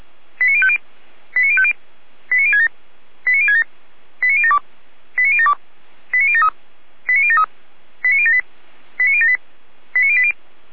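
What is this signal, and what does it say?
Five-tone selective call (ZVEI), heard as quick runs of five tones, tones of about 80 ms. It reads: 89050 89050 89067 89067 89072 89072 89073 89073 89078 89078 89080
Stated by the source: ZVEI five-tone selective call received by SDR.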